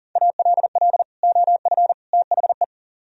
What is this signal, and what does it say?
Morse code, heard as a keyed tone, about 55 wpm, spaced as for 48 wpm, 700 Hz. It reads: ALL OF THE
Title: Speed-Racing - Sets of 3 Words 40wpm